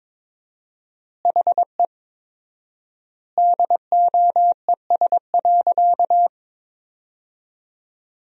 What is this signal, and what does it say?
Morse code, heard as a keyed tone, about 22 wpm, 700 Hz. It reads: HE DOES.
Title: 4x Word-Spacing - Sentences from Top 500 Words - 22wpm